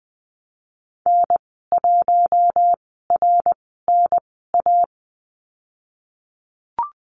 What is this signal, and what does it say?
Morse code, heard as a keyed tone, about 20 wpm, spaced as for 15 wpm, 700 Hz. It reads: N1RNA
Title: 062 Lesson 11g - Letter h - Call Signs 20wpm